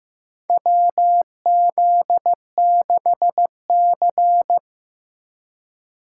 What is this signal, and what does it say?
Morse code, heard as a keyed tone, about 15 wpm, 700 Hz. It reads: WZ6C
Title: USA Call Signs - 4 Characters - 3x Character Spacing - 15wpm